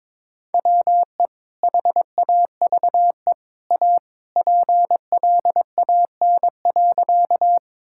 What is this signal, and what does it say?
Morse code, heard as a keyed tone, about 22 wpm, 700 Hz. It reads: WE HAVE A PLAN.
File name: practice / 22wpm / Sentences with 2-Word Phrases - 22wpm